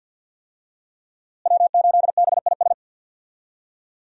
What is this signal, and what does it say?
Morse code, heard as a keyed tone, about 52 wpm, 700 Hz. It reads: W8BIS
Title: USA Call Signs - 5 Characters - Repeated 3x - 50wpm